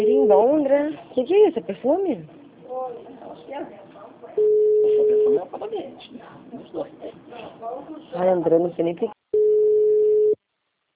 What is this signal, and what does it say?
Ringback tone (Europe, 425 Hz), with speech.